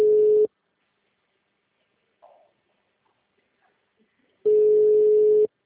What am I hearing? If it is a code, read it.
Ringback tone (Europe, 425 Hz).